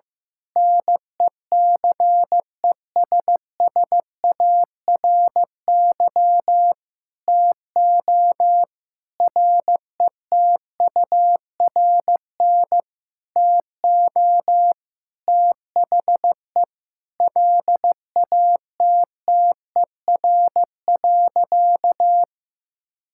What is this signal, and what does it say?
Morse code, heard as a keyed tone, about 15 wpm, 700 Hz. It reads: NECESSARY TO RETURN TO THE LATTER.
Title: Sentences from Top 1600 Words 15wpm